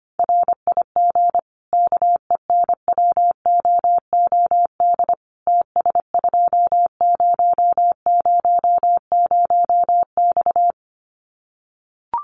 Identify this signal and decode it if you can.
Morse code, heard as a keyed tone, about 25 wpm, 700 Hz. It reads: RIG KENWOOD TS2000X
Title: Common QSO Phrases 25wpm